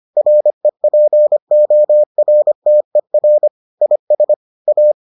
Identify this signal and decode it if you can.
Morse code, heard as a keyed tone, about 25 wpm, 600 Hz. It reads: REPORTER IS A